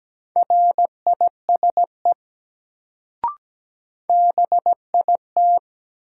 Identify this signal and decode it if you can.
Morse code, heard as a keyed tone, about 17 wpm, 700 Hz. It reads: RISE BIT
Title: Sentences from Top 900 Words - New Word Drill - 17wpm